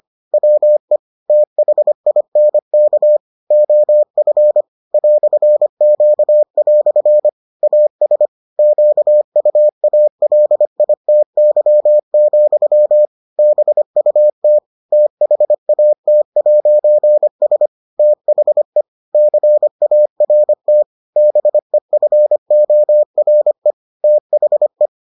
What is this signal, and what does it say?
Morse code, heard as a keyed tone, about 25 wpm, 600 Hz. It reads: WE THINK OF "Q" AS QUALITY, BUT THAT'S THE CART BEFORE THE